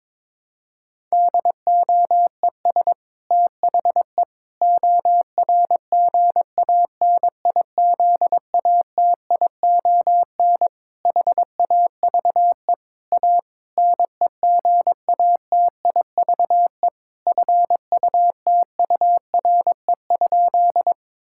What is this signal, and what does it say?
Morse code, heard as a keyed tone, about 22 wpm, 700 Hz. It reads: DOES THE ORGANIZATION HAVE A NEGATIVE FUTURE?
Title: Sentences from Top 1000 Words - Review Part 6 - 22wpm